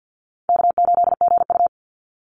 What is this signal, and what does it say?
Morse code, heard as a keyed tone, about 50 wpm, 700 Hz. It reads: K8GU
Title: Easy Call Signs - 50wpm